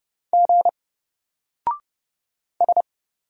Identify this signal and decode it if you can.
Morse code, heard as a keyed tone, about 29 wpm, 700 Hz. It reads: G S
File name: Single-Letter-Number-30wpm